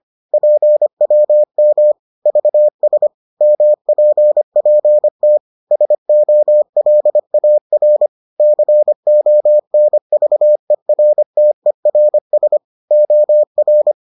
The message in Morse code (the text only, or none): PWM VS MPPT SOLAR CONVERTERS OR